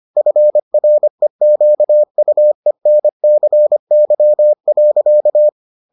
Morse code, heard as a keyed tone, about 25 wpm, 600 Hz. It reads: FREQUENCY.